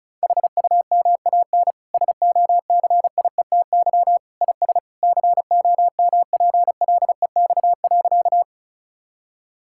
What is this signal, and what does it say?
Morse code, heard as a keyed tone, about 34 wpm, 700 Hz. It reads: HUMAN SOCIETY IS COMPLEX.
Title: Sentences from Top 1000 Words - Review Part 6 - 35wpm